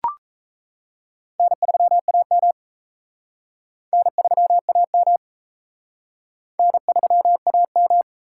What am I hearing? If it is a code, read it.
Morse code, heard as a keyed tone, about 36 wpm, 700 Hz. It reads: N3AM N3AM N3AM